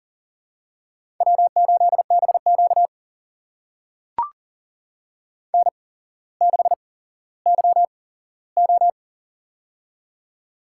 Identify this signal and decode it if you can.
Morse code, heard as a keyed tone, about 40 wpm, spaced as for 17 wpm, 700 Hz. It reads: W8BQ N6YO
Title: USA Call Signs - 4 Characters - 8x Character Spacing - 40wpm